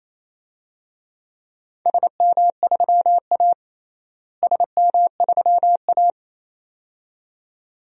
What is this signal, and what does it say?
Morse code, heard as a keyed tone, about 28 wpm, 700 Hz. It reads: SM3A SM3A
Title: DXCC-100 Call Signs - 4 Characters - Repeated 2x - 28wpm